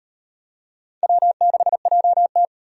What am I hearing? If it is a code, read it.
Morse code, heard as a keyed tone, about 35 wpm, 700 Hz. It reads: W6JT